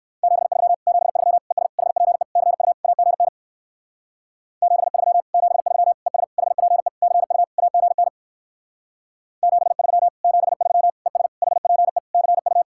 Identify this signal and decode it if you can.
Morse code, heard as a keyed tone, about 58 wpm, 700 Hz. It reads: 73 73 ES HPE CU AGN 73 73 ES HPE CU AGN 73 73 ES HPE CU